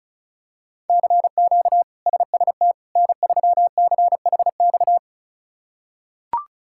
Morse code, heard as a keyed tone, about 36 wpm, 700 Hz. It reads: CQ SST N3CHX
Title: K1USN SST QSO Practice - 35wpm